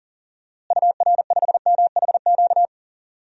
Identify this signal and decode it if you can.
Morse code, heard as a keyed tone, about 40 wpm, 700 Hz. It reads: UR5MHQ
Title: DXCC-100 Call Signs - 6 Characters - Repeated 3x - 40wpm